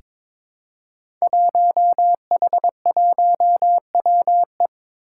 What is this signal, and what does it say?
Morse code, heard as a keyed tone, about 23 wpm, 700 Hz. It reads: 1H1WE
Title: Five Challenging Letter-Number Combinations 22wpm